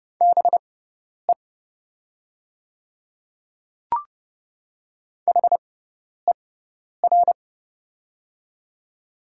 Morse code, heard as a keyed tone, about 30 wpm, spaced as for 10 wpm, 700 Hz. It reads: BE HER
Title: 6x Character Spacing - Top 100 Words - 30wpm